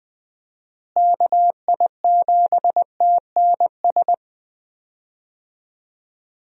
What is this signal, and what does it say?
Morse code, heard as a keyed tone, about 20 wpm, 700 Hz. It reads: KI7TNS
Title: USA Call Signs - 6 Characters - Repeated 3x - 20wpm